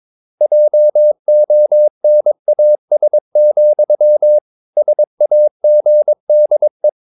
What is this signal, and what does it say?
Morse code, heard as a keyed tone, about 22 wpm, 600 Hz. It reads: JONAS, SAGDE
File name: Ferie QRS 21wpm